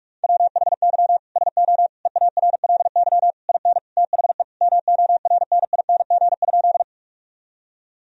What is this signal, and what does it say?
Morse code, heard as a keyed tone, about 46 wpm, 700 Hz. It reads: WHY SO EARLY IN THE MORNING?
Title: Sentences from Top 1000 Words - Review Part 2 - 45wpm